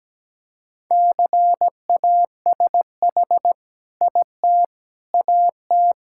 Morse code, heard as a keyed tone, about 17 wpm, 700 Hz. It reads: CASH IT AT
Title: Sentences from Top 1000 Words - Review Part 5 - 17wpm